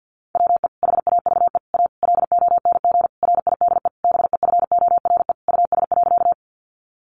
Morse code, heard as a keyed tone, about 50 wpm, 700 Hz. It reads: WE HAVE A LONG RIDE BEFORE US.
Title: Sentences from Top 1200 Words 50wpm